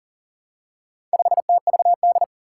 Morse code, heard as a keyed tone, about 37 wpm, 700 Hz. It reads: 5TVD
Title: Four Challenging Letter-Number Combinations 40wpm